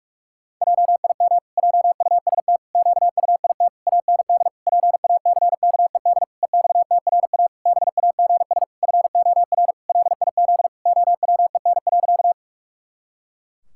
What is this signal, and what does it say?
Morse code, heard as a keyed tone, about 43 wpm, 700 Hz. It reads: JIM JUST QUIT AND PACKED EXTRA BAGS FOR LIZ OWEN.
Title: Pangram Challenge v2 - 45wpm